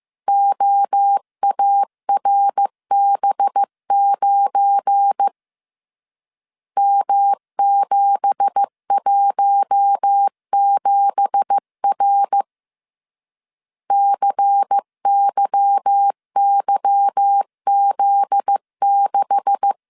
Morse code, heard as a keyed tone, about 15 wpm, 800 Hz. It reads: OARB9 M717R CYYZ6
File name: CW / DET / 1 9 1500C S A 14wpm 3222